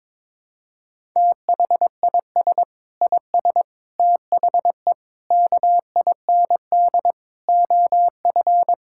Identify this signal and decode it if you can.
Morse code, heard as a keyed tone, about 22 wpm, 700 Hz. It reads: THIS IS THE KIND OF
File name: practice / 22wpm / Sets of 7 Words - 22wpm